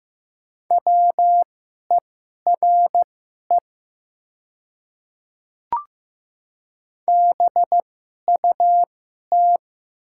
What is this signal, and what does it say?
Morse code, heard as a keyed tone, about 15 wpm, spaced as for 11 wpm, 700 Hz. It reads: WERE BUT
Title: 2x Character Spacing - Top 100 Words - 15wpm